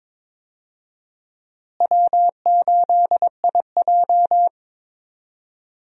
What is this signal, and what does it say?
Morse code, heard as a keyed tone, about 22 wpm, 700 Hz. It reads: W8IJ
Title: USA Call Signs - 4 Characters - Repeated 2x - 22wpm